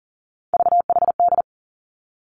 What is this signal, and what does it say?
Morse code, heard as a keyed tone, about 39 wpm, 700 Hz. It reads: VHD
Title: Three Challenging Letter-Number Combinations 40wpm